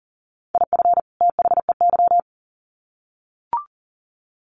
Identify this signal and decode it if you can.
Morse code, heard as a keyed tone, about 40 wpm, 700 Hz. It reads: IF THEY
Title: Sets of 2 Words 40wpm